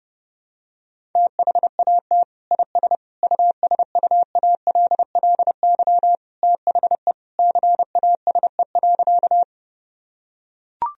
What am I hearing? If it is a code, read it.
Morse code, heard as a keyed tone, about 32 wpm, 700 Hz. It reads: THAT IS USUALLY THE CASE.
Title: Sentences from Top 1000 Words - Review Part 3 - 30wpm